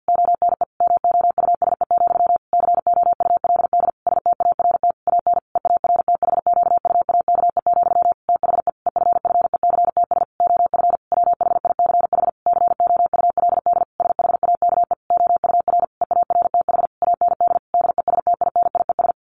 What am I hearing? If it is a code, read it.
Morse code, heard as a keyed tone, about 50 wpm, 700 Hz. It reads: ONE MOUSE, COULD START AN EARTHQUAKE, THE EFFECTS OF WHICH COULD SHAKE OUR EARTH AND DESTINIES
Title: A Sound Of Thunder 50wpm